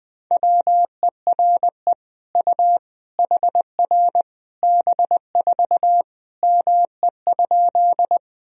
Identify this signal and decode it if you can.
Morse code, heard as a keyed tone, about 20 wpm, 700 Hz. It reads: WERE U HR B4 ME?
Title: Sentences from Top 200 Words and Abbreviations 20wpm